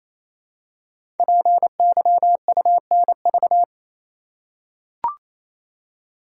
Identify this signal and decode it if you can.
Morse code, heard as a keyed tone, about 28 wpm, 700 Hz. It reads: PYUNV
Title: Code Groups - 5 Letters - 28wpm